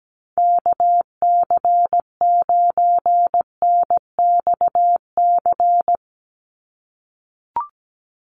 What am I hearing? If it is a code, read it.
Morse code, heard as a keyed tone, about 17 wpm, 700 Hz. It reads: KC9NXC
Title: USA Call Signs 17wpm